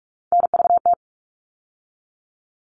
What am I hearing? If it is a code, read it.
Morse code, heard as a keyed tone, about 44 wpm, 700 Hz. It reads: NVT
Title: All Cut Numbers - 3-Digit 45wpm